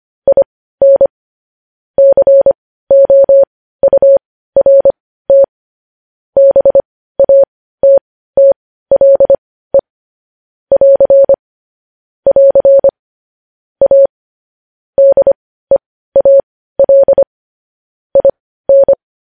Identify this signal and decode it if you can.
Morse code, heard as a keyed tone, about 25 wpm, spaced as for 15 wpm, 550 Hz. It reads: IN COURT BATTLE <AR> <AR> A DEAL IN
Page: news-headlines-in-morse-code-at-15-wpm